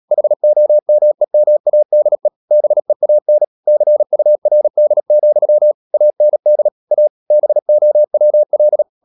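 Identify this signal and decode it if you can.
Morse code, heard as a keyed tone, about 37 wpm, 600 Hz. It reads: HOMEMADE BEAN CURD, AND A BOWL
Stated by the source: Morse code practice at 35–39 wpm sending English text of Ken Follett's novel Never, from CW med Gnister.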